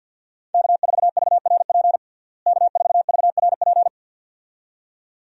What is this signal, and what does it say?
Morse code, heard as a keyed tone, about 45 wpm, 700 Hz. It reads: K4VRP K4VRP
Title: USA Call Signs - 5 Characters - Repeated 2x - 50wpm